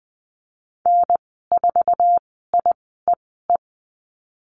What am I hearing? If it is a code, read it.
Morse code, heard as a keyed tone, about 20 wpm, spaced as for 14 wpm, 700 Hz. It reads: N4IEE
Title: 062 Lesson 11g - Letter h - Call Signs 20wpm